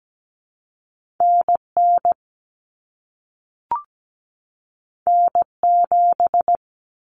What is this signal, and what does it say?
Morse code, heard as a keyed tone, about 17 wpm, 700 Hz. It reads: NN N7
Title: Common Cut Numbers - 2-Digit 17wpm